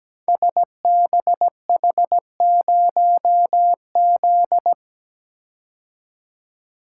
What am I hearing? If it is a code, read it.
Morse code, heard as a keyed tone, about 17 wpm, 690 Hz. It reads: SBH0Z
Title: Code Groups - 5 Letter:Numbers - 17wpm